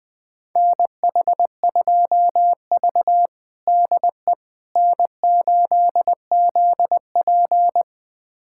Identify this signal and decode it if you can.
Morse code, heard as a keyed tone, about 20 wpm, 700 Hz. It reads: NH2V DE N8ZP
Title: USA Callsign DE Callsign - 4 Characters - 20wpm